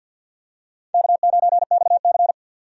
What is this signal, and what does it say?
Morse code, heard as a keyed tone, about 46 wpm, 700 Hz. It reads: K9XC
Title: USA Call Signs - 4 Characters - 50wpm